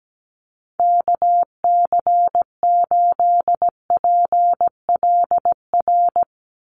Morse code, heard as a keyed tone, about 17 wpm, 700 Hz. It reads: KC8PLR
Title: USA Call Signs 17wpm